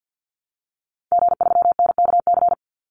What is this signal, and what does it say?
Morse code, heard as a keyed tone, about 50 wpm, 700 Hz. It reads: G3NKC
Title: Call Signs Repeated 2x - 50wpm